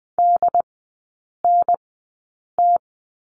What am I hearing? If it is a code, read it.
Morse code, heard as a keyed tone, about 19 wpm, spaced as for 8 wpm, 690 Hz. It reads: DNT
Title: 087 Lesson 15e - Number 5 - Three Characters 20wpm